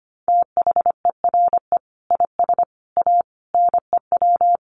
Morse code, heard as a keyed tone, about 25 wpm, 700 Hz. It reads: THERE IS A NEW
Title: Sentences from Top 1500 Words 25wpm